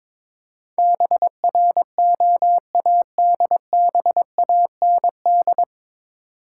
Morse code, heard as a keyed tone, about 23 wpm, 700 Hz. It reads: BROADBAND
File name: 9 Letter Words - First 3 Letters Given - 22wpm